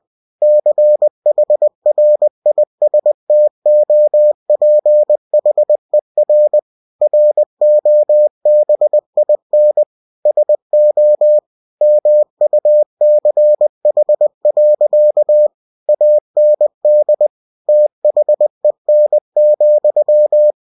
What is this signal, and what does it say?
Morse code, heard as a keyed tone, about 20 wpm, 600 Hz. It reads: CHRISTOPHER ROBIN SO MUCH. AND THEN,
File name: cw-books / winnie-the-pooh / winnie-20wpm-page-0023